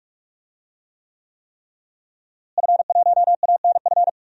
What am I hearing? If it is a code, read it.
Morse code, heard as a keyed tone, about 41 wpm, 700 Hz. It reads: F1ANF
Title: DXCC-100 Call Signs - 5 Characters - Repeated 3x - 45wpm